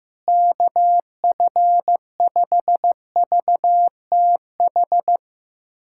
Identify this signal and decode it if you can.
Morse code, heard as a keyed tone, about 15 wpm, 700 Hz. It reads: KF5VTH
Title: USA Callsign DE Callsign - 6 Characters - 15wpm